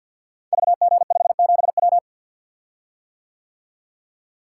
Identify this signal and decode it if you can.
Morse code, heard as a keyed tone, about 50 wpm, 700 Hz. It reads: VG57W
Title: Code Groups - 5 Letter:Numbers - 50wpm